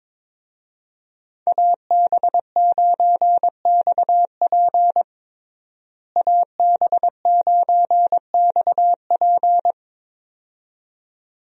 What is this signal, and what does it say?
Morse code, heard as a keyed tone, about 22 wpm, 700 Hz. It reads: AB9XP AB9XP
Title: USA Call Signs - 5 Characters - Repeated 2x - 22wpm